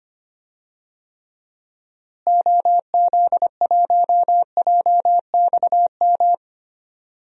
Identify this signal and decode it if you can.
Morse code, heard as a keyed tone, about 25 wpm, 700 Hz. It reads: OZ1JXM
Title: DXCC-100 Call Signs - 6 Characters - Repeated 2x - 25wpm